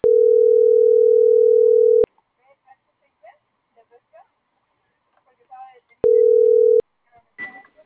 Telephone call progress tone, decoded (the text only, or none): ringback tone